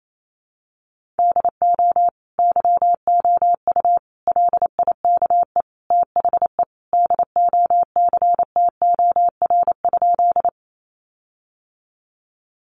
Morse code, heard as a keyed tone, about 28 wpm, 700 Hz. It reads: DO YOU LIKE THE DOCTOR?
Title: Sentences from Top 1500 Words 28wpm